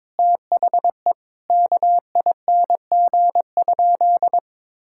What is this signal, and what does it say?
Morse code, heard as a keyed tone, about 22 wpm, 700 Hz. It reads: THE KING?
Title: Sentences from Top 1000 Words - Review Part 6 - 22wpm